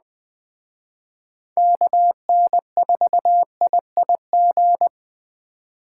Morse code, heard as a keyed tone, about 20 wpm, 700 Hz. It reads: KN4IIG
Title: Speed Racing - USA 2x3 Call Signs 17wpm